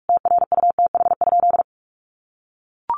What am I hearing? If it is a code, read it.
Morse code, heard as a keyed tone, about 44 wpm, 700 Hz. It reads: TRUTH?
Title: Sentences from Top 1400 Words 45wpm